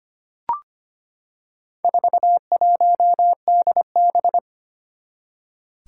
Morse code, heard as a keyed tone, about 25 wpm, 700 Hz. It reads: 41DB
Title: Four Challenging Letter-Number Combinations 25wpm